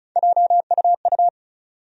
Morse code, heard as a keyed tone, about 36 wpm, 700 Hz. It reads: JUU
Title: USA Call Sign Postfixes - 3 Letters 35wpm